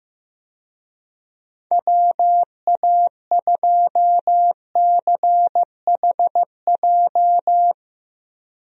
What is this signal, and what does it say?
Morse code, heard as a keyed tone, about 15 wpm, 700 Hz. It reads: WA2CHJ
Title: USA Call Signs - 6 Characters - 15wpm